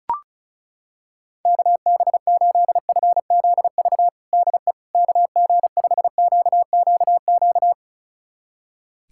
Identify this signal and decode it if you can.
Morse code, heard as a keyed tone, about 36 wpm, 700 Hz. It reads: KB8FZV DE KG5QQQ
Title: USA Callsign DE Callsign - 6 Characters - 35wpm